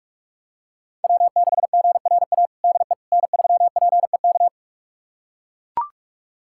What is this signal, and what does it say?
Morse code, heard as a keyed tone, about 41 wpm, 700 Hz. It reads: W6GRA DE N3PEK
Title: USA Callsign DE Callsign 45wpm